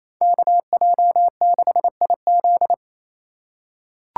Morse code, heard as a keyed tone, about 28 wpm, 700 Hz. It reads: KJ6IZ
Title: USA Call Signs - 3x Character Spacing - 28wpm